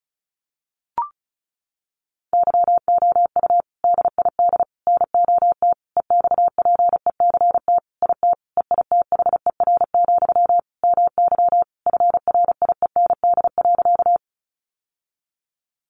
Morse code, heard as a keyed tone, about 34 wpm, 700 Hz. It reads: YOU DID NOT EXPECT IT EITHER, MY FRIEND.
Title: Sentences from Top 1100 Words 35wpm